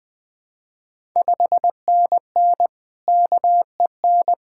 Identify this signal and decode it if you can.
Morse code, heard as a keyed tone, about 20 wpm, 700 Hz. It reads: 5NN KEN
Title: Ontario QSO Party Exchange 20wpm